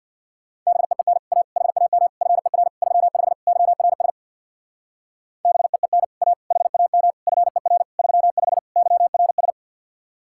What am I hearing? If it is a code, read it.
Morse code, heard as a keyed tone, about 62 wpm, 700 Hz. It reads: BEEN A HAM FER 35 YRS BEEN A HAM FER 35 YRS
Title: Speed-Racing Common QSO Phrases Encore - 50wpm